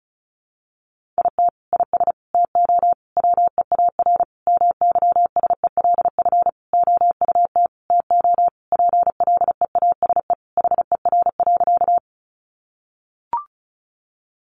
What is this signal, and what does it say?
Morse code, heard as a keyed tone, about 34 wpm, 700 Hz. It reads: IT IS TO WEAR MYSELF OUT TO PLEASE HER.
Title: Sentences from Top 1500 Words 35wpm